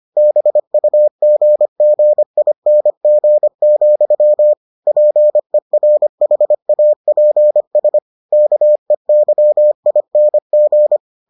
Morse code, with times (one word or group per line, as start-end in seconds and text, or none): 0.2-4.5 BUGGING,
4.9-8.0 PERHAPS
8.3-11.0 KEYING